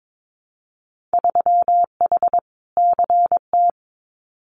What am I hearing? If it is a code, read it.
Morse code, heard as a keyed tone, about 22 wpm, 700 Hz. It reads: 3H CT
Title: Winter Field Day Exchanges Encore 22wpm